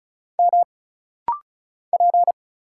Morse code, heard as a keyed tone, about 36 wpm, 700 Hz. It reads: M P
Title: Single Letter-Number - Rapid-Fire 35wpm